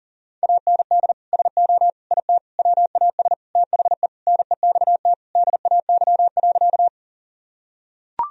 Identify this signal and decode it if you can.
Morse code, heard as a keyed tone, about 40 wpm, 700 Hz. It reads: AND SO IT WAS THE NEXT DAY.
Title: Sentences from Top 1000 Words - Review Part 1 - 40wpm